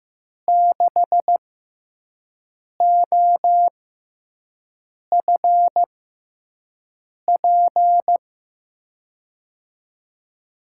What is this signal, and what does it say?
Morse code, heard as a keyed tone, about 15 wpm, spaced as for 5 wpm, 700 Hz. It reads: 6OFP
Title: USA Call Signs - 5 Characters - 6x Character Spacing - 15wpm